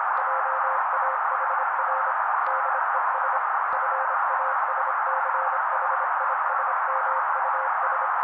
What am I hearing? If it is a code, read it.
Morse code, heard as a keyed tone, about 26 wpm, 530 Hz. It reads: JAHR DES FASCHISMUS
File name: daily / 25wpm noise